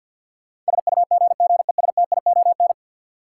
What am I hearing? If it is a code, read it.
Morse code, heard as a keyed tone, about 47 wpm, 700 Hz. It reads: SUGGESTION
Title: Sentences from Top 1600 Words - New Word Drill - 50wpm